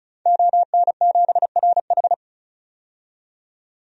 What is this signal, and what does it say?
Morse code, heard as a keyed tone, about 35 wpm, 700 Hz. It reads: ON7RH
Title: DXCC-100 Call Signs - 5 Characters - 35wpm